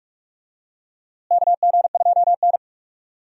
Morse code, heard as a keyed tone, about 46 wpm, 700 Hz. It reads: KG2N
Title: USA Call Signs - 4 Characters - Repeated 3x - 45wpm